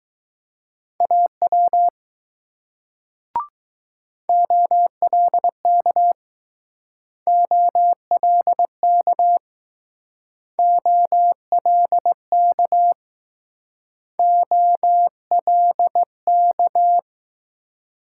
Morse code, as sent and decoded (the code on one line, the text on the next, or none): .- .-- / --- .-.. -.- / --- .-.. -.- / --- .-.. -.- / --- .-.. -.-
AW OLK OLK OLK OLK